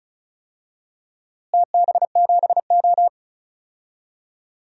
Morse code, heard as a keyed tone, about 36 wpm, 700 Hz. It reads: TB7O
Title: DXCC-100 Call Signs - 4 Characters - Repeated 3x - 35wpm